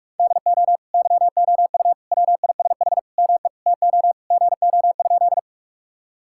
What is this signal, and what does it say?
Morse code, heard as a keyed tone, about 46 wpm, 700 Hz. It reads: DO YOU WISH ME TO GO?
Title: Sentences from Top 1600 Words 45wpm